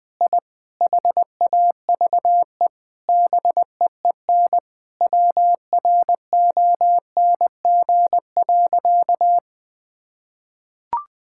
Morse code, heard as a keyed tone, about 21 wpm, 700 Hz. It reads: I HAVE BEEN WRONG.
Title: Sentences with 9-Word Phrases - Spoken Beginning - 20wpm